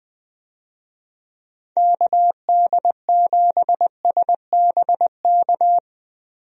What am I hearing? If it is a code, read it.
Morse code, heard as a keyed tone, about 20 wpm, 700 Hz. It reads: KD7SBK
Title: USA Call Signs - 6 Characters - Repeated 3x - 20wpm